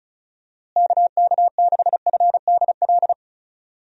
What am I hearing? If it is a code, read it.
Morse code, heard as a keyed tone, about 34 wpm, 700 Hz. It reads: KK6FDL
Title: USA Call Signs - 6 Characters - Repeated 3x - 35wpm